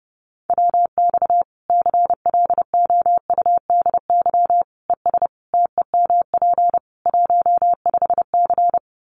Morse code, heard as a keyed tone, about 30 wpm, 700 Hz. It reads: WX CLOUDY ES TEMP 15C
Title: Common QSO Phrases 30wpm